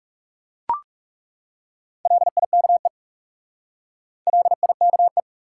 Morse code, heard as a keyed tone, about 39 wpm, 700 Hz. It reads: LIKE LIKE